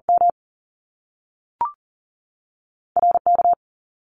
Morse code, heard as a keyed tone, about 40 wpm, 700 Hz. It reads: M RK
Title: Top Two Letter Combinations in Words 40wpm